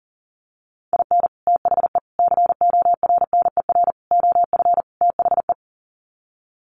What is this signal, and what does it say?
Morse code, heard as a keyed tone, about 40 wpm, 700 Hz. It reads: IN THE CORNER OF THE
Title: Sets of 5 Words Encore 40wpm